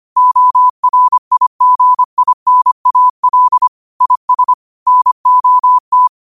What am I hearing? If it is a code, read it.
Morse code, heard as a keyed tone, about 25 wpm, 1000 Hz. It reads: ORIGINAL IS NOT
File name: 25wpm